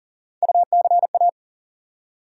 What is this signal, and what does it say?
Morse code, heard as a keyed tone, about 40 wpm, 700 Hz. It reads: UCA
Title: USA Call Sign Postfixes - 3 Letters 40wpm